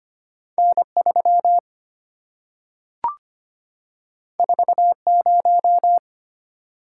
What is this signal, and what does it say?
Morse code, heard as a keyed tone, about 25 wpm, 700 Hz. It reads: N3 40
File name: Two Letter-Number Combinations - 25wpm